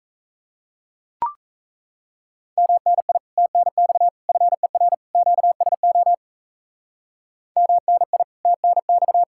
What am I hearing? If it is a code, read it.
Morse code, heard as a keyed tone, about 35 wpm, 700 Hz. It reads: MNI TNX FER QSO MNI TNX
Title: Speed-Racing Common QSO Phrases - 28wpm